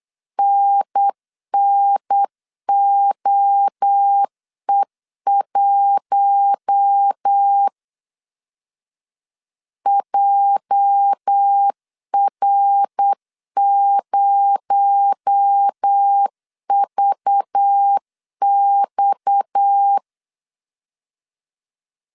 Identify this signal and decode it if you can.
Morse code, heard as a keyed tone, about 8 wpm, 800 Hz. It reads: NNOE1 JR0VX
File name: CW / DET / 1 9 1500C S C 8wpm 5653